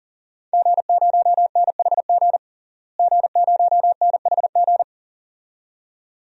Morse code, heard as a keyed tone, about 40 wpm, 700 Hz. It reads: G0NHG G0NHG
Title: DXCC-100 Call Signs - 5 Characters - Repeated 2x - 40wpm